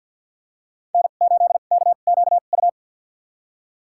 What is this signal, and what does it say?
Morse code, heard as a keyed tone, about 46 wpm, spaced as for 34 wpm, 700 Hz. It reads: N8KQU
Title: USA Call Signs - 5 Characters - 2x Character Spacing - 50wpm